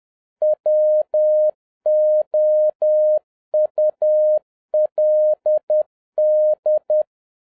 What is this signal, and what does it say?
Morse code, heard as a keyed tone, about 10 wpm, 610 Hz. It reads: WOULD